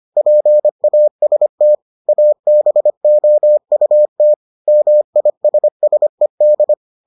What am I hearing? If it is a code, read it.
Morse code, heard as a keyed tone, about 25 wpm, 600 Hz. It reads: PAST ABOUT MISSED